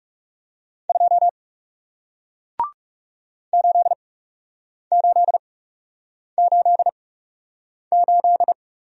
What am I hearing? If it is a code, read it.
Morse code, heard as a keyed tone, about 40 wpm, spaced as for 8 wpm, 700 Hz. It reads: 2 8888